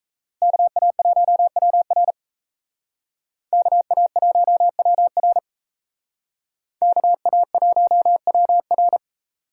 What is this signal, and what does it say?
Morse code, heard as a keyed tone, about 36 wpm, 700 Hz. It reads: KA1WR KA1WR KA1WR